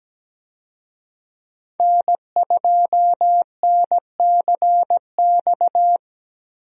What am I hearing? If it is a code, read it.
Morse code, heard as a keyed tone, about 17 wpm, 700 Hz. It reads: N2NCX
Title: USA Call Signs - 5 Characters - Repeated 2x - 17wpm